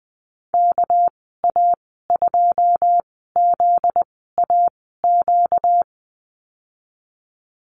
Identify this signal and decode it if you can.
Morse code, heard as a keyed tone, about 20 wpm, spaced as for 15 wpm, 700 Hz. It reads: KA2ZAQ
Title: 245 Lesson 39g - Letter z - Call Signs 20wpm